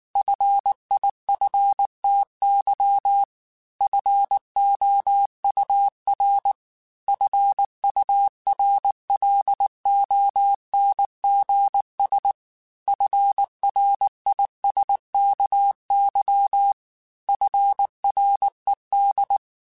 Morse code, heard as a keyed tone, about 19 wpm, 800 Hz. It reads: FIFTY FOUR FURLONGS FRISKY FRED